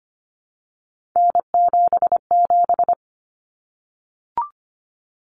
Morse code, heard as a keyed tone, about 25 wpm, 700 Hz. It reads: N77
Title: Common Cut Numbers - 3-Digit 25wpm